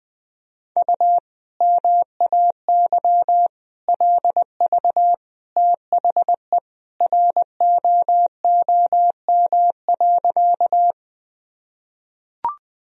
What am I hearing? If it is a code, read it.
Morse code, heard as a keyed tone, about 20 wpm, 700 Hz. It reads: U MAY LV THE ROOM.
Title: Sentences from Top 400 Words and Abbreviations 20wpm